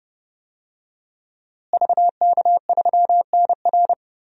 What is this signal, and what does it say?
Morse code, heard as a keyed tone, about 30 wpm, 700 Hz. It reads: VK3NR
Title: DXCC-100 Call Signs - 5 Characters - Repeated 3x - 30wpm